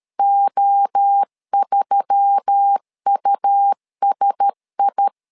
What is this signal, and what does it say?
Morse code, heard as a keyed tone, about 13 wpm, 800 Hz. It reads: O3USI
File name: LEZIONI / 4 Q1S3 12wpm 2023